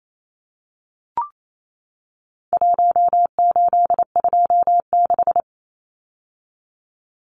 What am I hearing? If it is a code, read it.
Morse code, heard as a keyed tone, about 28 wpm, 700 Hz. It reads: 1826
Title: Numbers - Contest Style - 4-Digit 28wpm